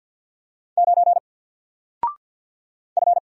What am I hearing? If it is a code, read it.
Morse code, heard as a keyed tone, about 48 wpm, 700 Hz. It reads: 9 F
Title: Single Character (Letter:Number:Punctuation:Prosign) Alt 50wpm